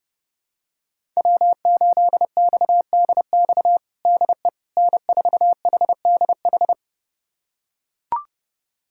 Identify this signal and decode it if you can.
Morse code, heard as a keyed tone, about 32 wpm, 700 Hz. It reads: W8XDX DE N4HDH
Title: USA Callsign DE Callsign - 5 Characters - 30wpm